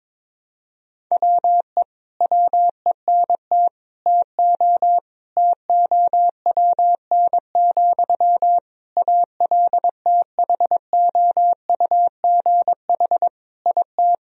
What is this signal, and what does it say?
Morse code, heard as a keyed tone, about 22 wpm, 700 Hz. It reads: WE WENT TO TOWN, ALTHOUGH IT
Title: Sentences from Top 1000 Words - Review Part 3 - 22wpm